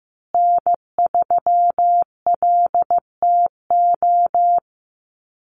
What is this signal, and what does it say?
Morse code, heard as a keyed tone, about 15 wpm, 700 Hz. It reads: N3LTO
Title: USA Call Signs Repeated 3x 15wpm